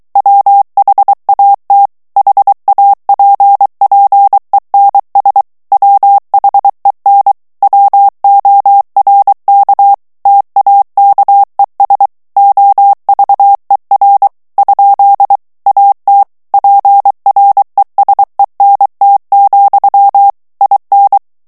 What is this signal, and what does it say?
Morse code, heard as a keyed tone, about 23 wpm, 800 Hz. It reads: WHAT HAPPENS WHEN WORK TAKES OVER? AT PRESENT, IN
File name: morse mp3 / ANGLAIS texte clair 1 vitesse 22 mots minute